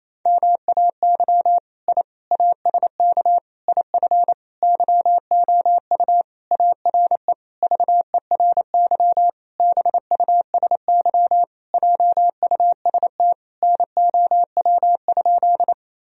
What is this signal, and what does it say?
Morse code, heard as a keyed tone, about 28 wpm, 700 Hz. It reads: MAY I ASK IF YOU ARE VERY BUSY JUST NOW?
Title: Sentences from Top 1000 Words - Review Part 3 - 28wpm